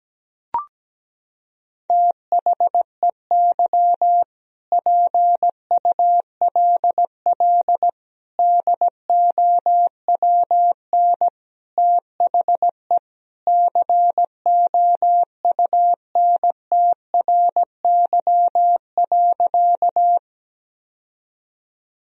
Morse code, heard as a keyed tone, about 17 wpm, 700 Hz. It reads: THEY PULL DOWN THE COUNTRY.
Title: Sentences from Top 1000 Words - Review Part 2 - 17wpm